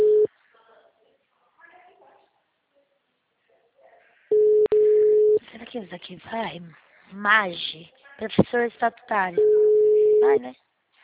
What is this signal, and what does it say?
Ringback tone (Europe, 425 Hz), with speech.